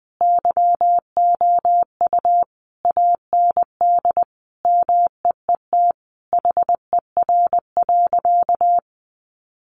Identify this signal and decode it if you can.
Morse code, heard as a keyed tone, about 20 wpm, 700 Hz. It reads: YOU AND MEET HER.
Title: Sentences from Top 1100 Words 20wpm